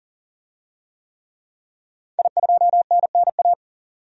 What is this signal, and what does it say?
Morse code, heard as a keyed tone, about 40 wpm, 700 Hz. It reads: I2NNA
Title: DXCC-100 Call Signs - 5 Characters - 40wpm